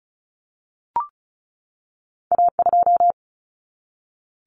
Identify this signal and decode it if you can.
Morse code, heard as a keyed tone, about 36 wpm, 700 Hz. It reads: A2
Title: Common Cut Numbers - 2-Digit 35wpm